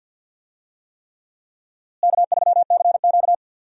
Morse code, heard as a keyed tone, about 47 wpm, 700 Hz. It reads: K3CQ
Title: USA Call Signs - 4 Characters - Repeated 3x - 50wpm